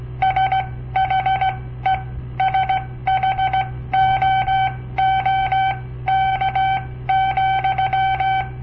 Morse code, heard as a keyed tone, about 17 wpm, spaced as for 14 wpm, 760 Hz. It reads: SHE SHOOK,
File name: papertapes / local / Code Practice-01